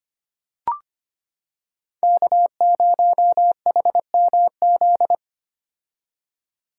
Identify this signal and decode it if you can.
Morse code, heard as a keyed tone, about 25 wpm, 700 Hz. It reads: K0HMZ
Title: K1USN SST QSO Practice - 25wpm